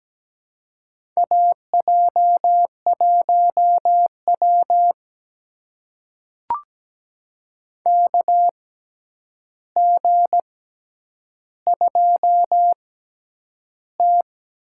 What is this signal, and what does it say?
Morse code, heard as a keyed tone, about 17 wpm, spaced as for 9 wpm, 700 Hz. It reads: AJ1W KG2T
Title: USA Call Signs - 4 Characters - 6x Character Spacing - 17wpm